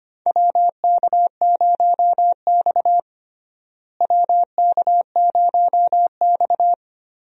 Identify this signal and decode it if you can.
Morse code, heard as a keyed tone, about 25 wpm, 700 Hz. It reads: WK0X WK0X
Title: USA Call Signs - 4 Characters - Repeated 2x - 25wpm